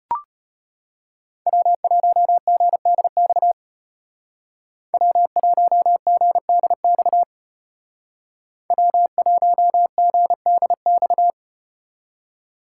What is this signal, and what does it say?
Morse code, heard as a keyed tone, about 34 wpm, 700 Hz. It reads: W1GDX W1GDX W1GDX